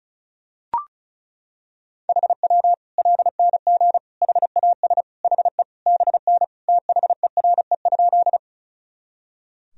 Morse code, heard as a keyed tone, about 34 wpm, 700 Hz. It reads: HW LNG HAS HE BN THERE?
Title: Sentences from Top 100 Words and Abbreviations 35wpm